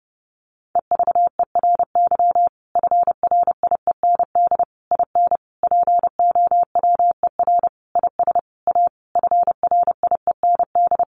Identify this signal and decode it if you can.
Morse code, heard as a keyed tone, about 30 wpm, 700 Hz. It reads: EVERY FRIEND IN POWER IS A FRIEND
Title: Sentences from Top 1400 Words 30wpm